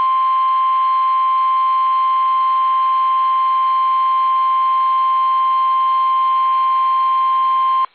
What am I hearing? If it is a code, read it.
Emergency Alert System (SAME) tones (the 1050 Hz attention tone). It is heard from 1.1 to 7.7 s.